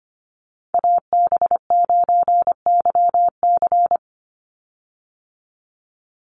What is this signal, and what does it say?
Morse code, heard as a keyed tone, about 25 wpm, 700 Hz. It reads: AB9YC
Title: Call Signs - 25wpm